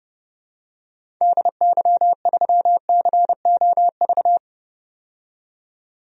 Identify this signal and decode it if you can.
Morse code, heard as a keyed tone, about 30 wpm, 690 Hz. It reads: DY3COV
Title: DXCC-100 Call Signs - 6 Characters - 30wpm